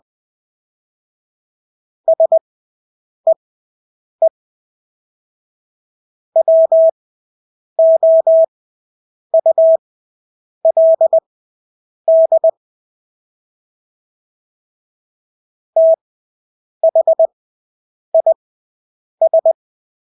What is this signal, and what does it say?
Morse code, heard as a keyed tone, about 20 wpm, spaced as for 8 wpm, 650 Hz. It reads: SEE WOULD THIS
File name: Engelske ord 20/8 wpm